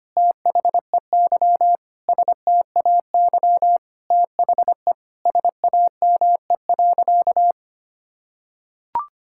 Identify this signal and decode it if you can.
Morse code, heard as a keyed tone, about 25 wpm, 700 Hz. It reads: THEY STAY THE SAME.
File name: practice / 25wpm / Sentences with 9-Word Phrases - 25wpm